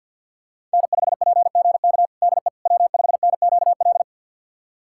Morse code, heard as a keyed tone, about 51 wpm, 700 Hz. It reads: N5PGK DE W5NQL
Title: USA Callsign DE Callsign - 5 Characters - 50wpm